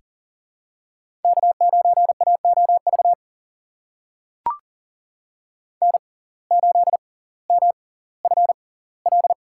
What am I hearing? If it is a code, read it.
Morse code, heard as a keyed tone, about 40 wpm, spaced as for 20 wpm, 700 Hz. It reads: K9AOV N8MFL
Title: USA Call Signs - 5 Characters - 6x Character Spacing - 40wpm